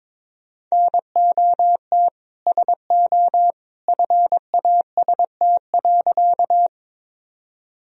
Morse code, heard as a keyed tone, about 22 wpm, 700 Hz. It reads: NOT SO FAST.
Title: Sentences from Top 1000 Words - Review Part 2 - 22wpm